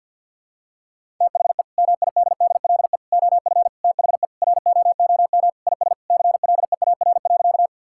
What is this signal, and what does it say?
Morse code, heard as a keyed tone, about 50 wpm, 700 Hz. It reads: THE MIDDLE OF THE ROOM IS CLEAR.
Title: Sentences from Top 1000 Words - Review Part 6 - 50wpm